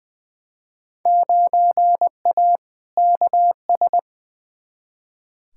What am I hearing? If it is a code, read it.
Morse code, heard as a keyed tone, about 21 wpm, 700 Hz. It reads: 9A KS
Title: Field Day Exchanges Encore - 20wpm